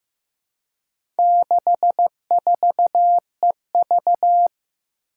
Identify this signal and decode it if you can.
Morse code, heard as a keyed tone, about 15 wpm, 700 Hz. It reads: 64EV
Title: Four Challenging Letter-Number Combinations 15wpm